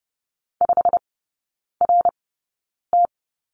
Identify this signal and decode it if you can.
Morse code, heard as a keyed tone, about 30 wpm, spaced as for 9 wpm, 700 Hz. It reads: 5RT
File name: course / 30:10wpm / 087 Lesson 15e - Number 5 - Three Characters 30wpm